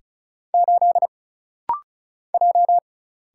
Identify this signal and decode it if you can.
Morse code, heard as a keyed tone, about 34 wpm, 700 Hz. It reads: 8 J
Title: Single Letter-Number - Rapid-Fire 35wpm